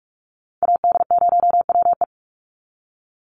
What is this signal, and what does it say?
Morse code, heard as a keyed tone, about 44 wpm, 700 Hz. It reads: AD0WE
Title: Call Signs - 45wpm